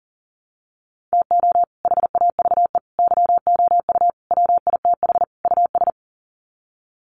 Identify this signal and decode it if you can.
Morse code, heard as a keyed tone, about 40 wpm, 700 Hz. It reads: TO HAVE YOU WITH US
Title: Sets of 5 Words 40wpm